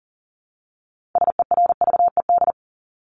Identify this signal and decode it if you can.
Morse code, heard as a keyed tone, about 40 wpm, 700 Hz. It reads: SERVED
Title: ED Suffix - Top 100 Words with Base Words 40wpm